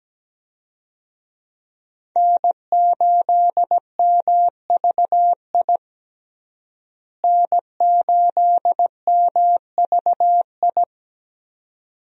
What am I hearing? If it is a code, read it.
Morse code, heard as a keyed tone, about 17 wpm, 700 Hz. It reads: N8MVI N8MVI
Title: USA Call Signs - 5 Characters - Repeated 2x - 17wpm